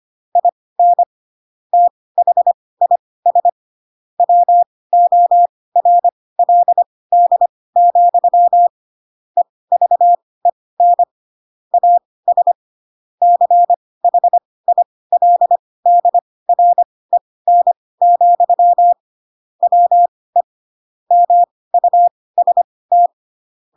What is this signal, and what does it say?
Morse code, heard as a keyed tone, about 25 wpm, spaced as for 18 wpm, 700 Hz. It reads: IN THIS WORLD, EVEN AS CHILDREN, WE MUST
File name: Pinocchio / Pinocchio ch7 12 25 ik1hge